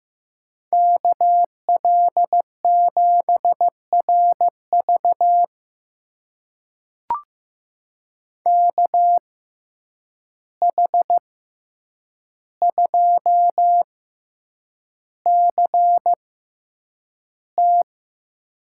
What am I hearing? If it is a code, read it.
Morse code, heard as a keyed tone, about 16 wpm, spaced as for 8 wpm, 700 Hz. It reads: KL7RV KH2CT
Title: USA Call Signs - 5 Characters - 6x Character Spacing - 15wpm